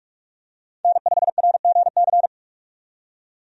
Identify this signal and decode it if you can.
Morse code, heard as a keyed tone, about 46 wpm, 700 Hz. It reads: N5RGC
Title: USA Call Signs - 5 Characters - 45wpm